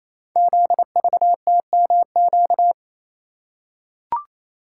Morse code, heard as a keyed tone, about 29 wpm, 700 Hz. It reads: ZVTMQ
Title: Code Groups - 5 Letters - 28wpm